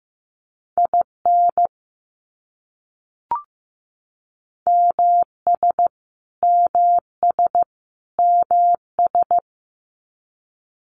Morse code, heard as a keyed tone, about 15 wpm, 700 Hz. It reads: IN MS MS MS
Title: US States Repeated 3x 15wpm